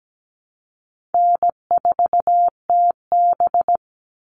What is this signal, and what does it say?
Morse code, heard as a keyed tone, about 17 wpm, 700 Hz. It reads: N4TB
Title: Easy Call Signs Repeated 2x - 17wpm